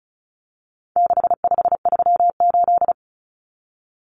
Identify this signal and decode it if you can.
Morse code, heard as a keyed tone, about 34 wpm, 700 Hz. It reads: 6538
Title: Common Cut Numbers - 4-Digit 35wpm